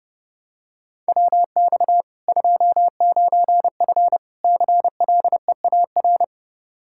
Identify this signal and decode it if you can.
Morse code, heard as a keyed tone, about 32 wpm, 700 Hz. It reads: WX 29F CLEAR
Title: Practice QSO - Exchange 2 - Info - 30wpm